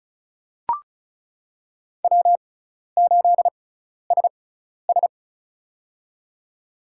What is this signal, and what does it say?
Morse code, heard as a keyed tone, about 34 wpm, spaced as for 12 wpm, 700 Hz. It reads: W8SS
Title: USA Call Signs - 4 Characters - 6x Character Spacing - 35wpm